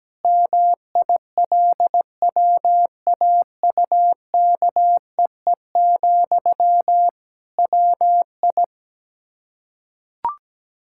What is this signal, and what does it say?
Morse code, heard as a keyed tone, about 17 wpm, 700 Hz. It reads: MILWAUKEE, WI
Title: Practice QSO - Exchange 1 - Beginning - 17wpm